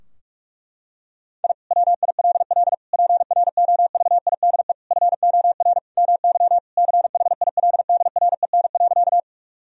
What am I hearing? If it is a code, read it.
Morse code, heard as a keyed tone, about 46 wpm, 700 Hz. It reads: I WILL PROVIDE FOR MY CHILDREN.
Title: Sentences from Top 1000 Words - Review Part 3 - 45wpm